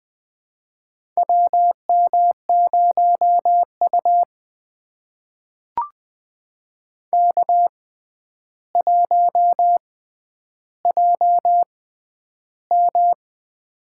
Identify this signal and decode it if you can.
Morse code, heard as a keyed tone, about 20 wpm, spaced as for 10 wpm, 700 Hz. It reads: WM0U K1JM
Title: USA Call Signs - 4 Characters - 6x Character Spacing - 20wpm